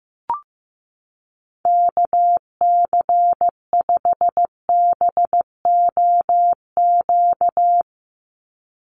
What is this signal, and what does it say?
Morse code, heard as a keyed tone, about 16 wpm, 700 Hz. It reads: KC5BOQ
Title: USA Call Signs 15wpm